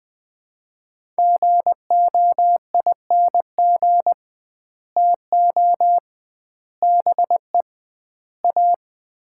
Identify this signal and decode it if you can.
Morse code, heard as a keyed tone, about 20 wpm, 700 Hz. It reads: GOING TO BE A
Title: 2x Word-Spacing - Sets of 4 Words - 20wpm